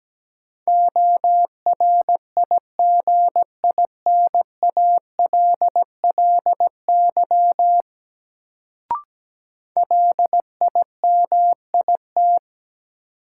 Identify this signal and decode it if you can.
Morse code, heard as a keyed tone, about 17 wpm, 700 Hz. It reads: ORIGINALLY LIMIT
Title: Sentences from Top 1500 Words - New Word Drill Alt - 17wpm